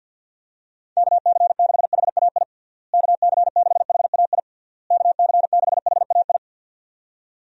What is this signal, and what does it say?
Morse code, heard as a keyed tone, about 52 wpm, 700 Hz. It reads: KC6HAI KC6HAI KC6HAI
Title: USA Call Signs - 6 Characters - Repeated 3x - 50wpm